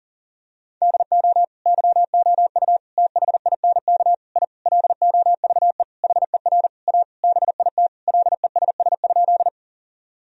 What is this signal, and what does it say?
Morse code, heard as a keyed tone, about 40 wpm, 700 Hz. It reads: DO YOU THINK I LOVE HER A BIT LESS?
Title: Sentences from Top 1000 Words - Review Part 2 - 40wpm